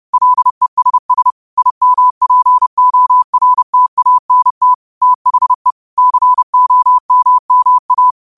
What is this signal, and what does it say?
Morse code, heard as a keyed tone, about 30 wpm, 1000 Hz. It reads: LESS IMPORTANT THE COMMA
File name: qotdfiles / 30wpm